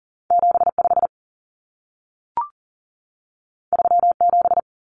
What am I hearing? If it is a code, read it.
Morse code, heard as a keyed tone, about 40 wpm, 700 Hz. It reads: 75 37
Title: Common Cut Numbers - 2-Digit 40wpm